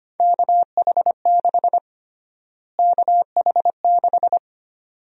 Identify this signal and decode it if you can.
Morse code, heard as a keyed tone, about 25 wpm, 700 Hz. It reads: KH6 KH6
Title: USA Call Sign Prefixes - Repeated 2x - 25wpm